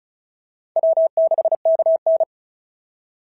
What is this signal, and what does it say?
Morse code, heard as a keyed tone, about 36 wpm, 690 Hz. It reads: W6KN
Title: SOTA QSO Practice 35wpm